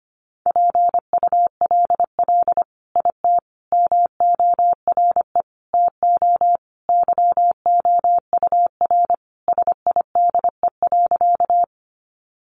Morse code, heard as a keyed tone, about 25 wpm, 700 Hz. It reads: PULL IT MORE TO YOUR SIDE.
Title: Sentences from Top 400 Words 25wpm